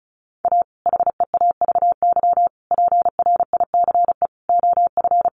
Morse code, heard as a keyed tone, about 34 wpm, 700 Hz. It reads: A HEAVY PRICE OF
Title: Sentences from Top 1100 Words 35wpm